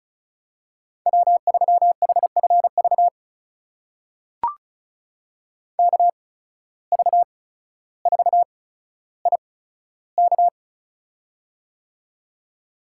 Morse code, heard as a keyed tone, about 36 wpm, spaced as for 15 wpm, 700 Hz. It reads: W3HFV KV4IK
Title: USA Call Signs - 5 Characters - 8x Character Spacing - 35wpm